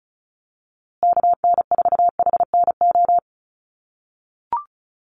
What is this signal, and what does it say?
Morse code, heard as a keyed tone, about 34 wpm, 700 Hz. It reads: KN4HNO
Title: USA Call Signs Repeated 3x 35wpm